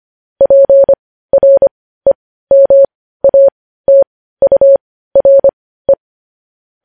Morse code, heard as a keyed tone, about 25 wpm, spaced as for 15 wpm, 550 Hz. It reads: PREMATURE
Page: news-headlines-in-morse-code-at-15-wpm